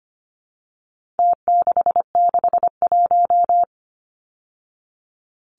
Morse code, heard as a keyed tone, about 25 wpm, 700 Hz. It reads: T661
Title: Numbers - Contest Style - 4-Digit 25wpm